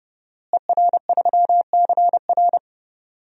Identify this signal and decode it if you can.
Morse code, heard as a keyed tone, about 29 wpm, 700 Hz. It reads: ER3CR
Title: DXCC-100 Call Signs - 5 Characters - Repeated 2x - 30wpm